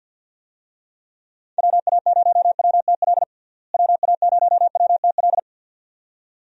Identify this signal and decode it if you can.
Morse code, heard as a keyed tone, about 50 wpm, 700 Hz. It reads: WA0WTL WA0WTL
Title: USA Call Signs - 6 Characters - Repeated 2x - 50wpm